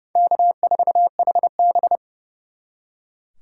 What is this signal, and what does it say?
Morse code, heard as a keyed tone, about 36 wpm, 700 Hz. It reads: K4HB
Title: Speed Racing - USA Call Signs 30wpm